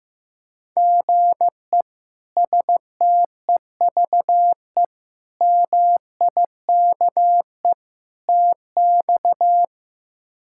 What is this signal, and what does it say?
Morse code, heard as a keyed tone, about 15 wpm, 700 Hz. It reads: GE STEVE MIKE TX
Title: K1USN SST QSO Practice - 15wpm